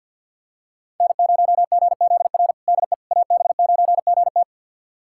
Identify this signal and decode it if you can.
Morse code, heard as a keyed tone, about 49 wpm, 700 Hz. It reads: N0GZR DE AB9GT
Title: USA Callsign DE Callsign - 5 Characters - 50wpm